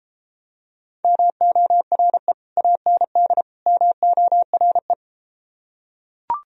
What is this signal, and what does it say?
Morse code, heard as a keyed tone, about 33 wpm, 700 Hz. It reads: MORE AND MORE